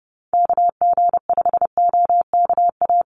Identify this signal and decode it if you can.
Morse code, heard as a keyed tone, about 30 wpm, 700 Hz. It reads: KG5OKA
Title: USA Call Signs 30wpm